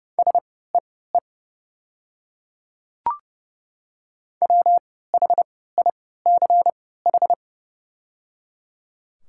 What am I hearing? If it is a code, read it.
Morse code, heard as a keyed tone, about 31 wpm, spaced as for 17 wpm, 700 Hz. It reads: SEE WHICH
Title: 3x Character Spacing - Top 200 Words - 30wpm